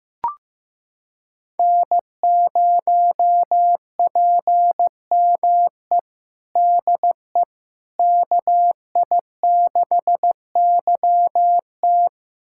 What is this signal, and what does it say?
Morse code, heard as a keyed tone, about 15 wpm, 700 Hz. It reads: N0PME DE KI6YT
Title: USA Callsign DE Callsign - 5 Characters - 15wpm